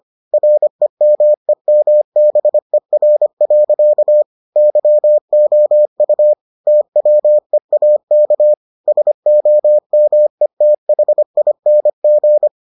Morse code, heard as a keyed tone, about 25 wpm, 600 Hz. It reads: REMEMBER. YOU TWEAK SOMETHING